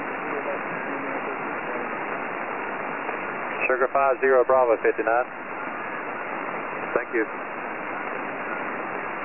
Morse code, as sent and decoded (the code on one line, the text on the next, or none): none